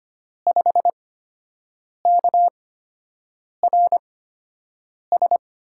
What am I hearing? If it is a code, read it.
Morse code, heard as a keyed tone, about 25 wpm, spaced as for 7 wpm, 700 Hz. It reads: 5KRS
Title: USA Call Signs - 5 Characters - 8x Character Spacing - 25wpm